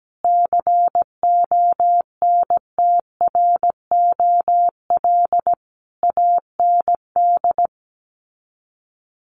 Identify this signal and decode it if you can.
Morse code, heard as a keyed tone, about 17 wpm, 700 Hz. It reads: CONTROL AND
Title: Sets of 5 Words Encore2 17wpm